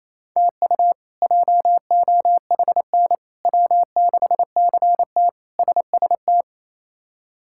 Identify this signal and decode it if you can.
Morse code, heard as a keyed tone, about 28 wpm, 700 Hz. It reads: TU JOHN W6CT SST
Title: K1USN SST QSO Practice - 28wpm